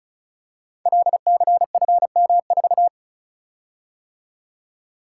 Morse code, heard as a keyed tone, about 36 wpm, 690 Hz. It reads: LCFM4
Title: Code Groups - 5 Letter:Numbers - 35wpm